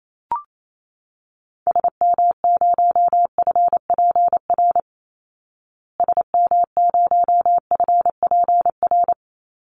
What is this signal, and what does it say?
Morse code, heard as a keyed tone, about 28 wpm, 700 Hz. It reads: SM0FPR SM0FPR
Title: Call Signs Repeated 3x - 28wpm